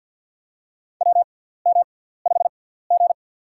Morse code, heard as a keyed tone, about 49 wpm, spaced as for 17 wpm, 700 Hz. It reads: WM5G
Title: USA Call Signs - 4 Characters - 6x Character Spacing - 50wpm